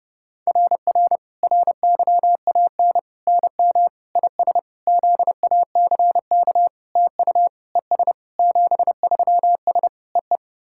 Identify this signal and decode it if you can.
Morse code, heard as a keyed tone, about 30 wpm, 700 Hz. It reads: RR RYAN NM IS ZACK TU ES 73S EE
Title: SOTA QSO Practice 30wpm